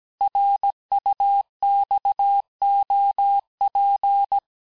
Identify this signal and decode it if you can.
Morse code, heard as a keyed tone, about 17 wpm, 800 Hz. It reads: RUXOP
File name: cours-du-lundi files / cours-1587-texte-chiffre-vitesse-16MPM